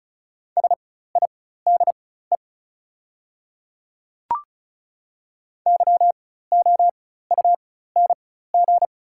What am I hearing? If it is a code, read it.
Morse code, heard as a keyed tone, about 35 wpm, spaced as for 16 wpm, 690 Hz. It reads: SIDE YOUNG
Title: 4x Character Spacing - Top 500 Words - 35wpm